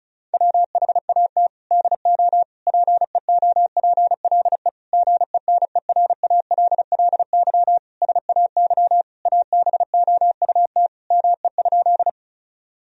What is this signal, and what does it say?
Morse code, heard as a keyed tone, about 36 wpm, 700 Hz. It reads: WHAT DO PEOPLE GENERALLY SAY ABOUT ME?
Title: Sentences from Top 1000 Words - Review Part 4 - 35wpm